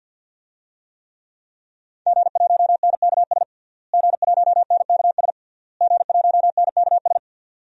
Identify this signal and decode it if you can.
Morse code, heard as a keyed tone, about 50 wpm, 700 Hz. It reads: G1NKS G1NKS G1NKS
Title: DXCC-100 Call Signs - 5 Characters - Repeated 3x - 50wpm